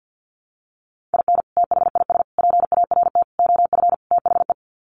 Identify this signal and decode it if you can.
Morse code, heard as a keyed tone, about 50 wpm, 700 Hz. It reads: IN THIS PART OF THE
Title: Sets of 5 Words 50wpm